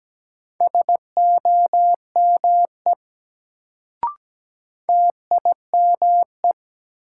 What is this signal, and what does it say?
Morse code, heard as a keyed tone, about 17 wpm, 700 Hz. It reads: SOME TIME
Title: Sentences from Top 100 Words - New Word Drill - 17wpm